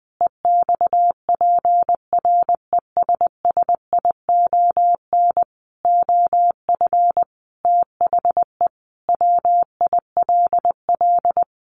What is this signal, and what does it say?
Morse code, heard as a keyed tone, about 20 wpm, 700 Hz. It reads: EXPRESSION OF THE WILL
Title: Sentences from Top 1500 Words 20wpm